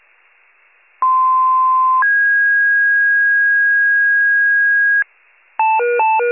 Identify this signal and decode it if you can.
Two-tone page, heard as a one-second tone, then a three-second tone, about 1040 then 1650 Hz. It is heard from 1.0 to 5.0 s.
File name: wcalert